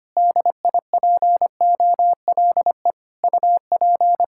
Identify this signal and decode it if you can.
Morse code, heard as a keyed tone, about 25 wpm, 700 Hz. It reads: DIPOLE UP
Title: Practice QSO - Exchange 2 - Info - 25wpm